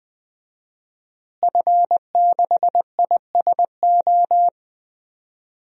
Morse code, heard as a keyed tone, about 20 wpm, 700 Hz. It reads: F6ISO
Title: DXCC-100 Call Signs - 5 Characters - Repeated 3x - 20wpm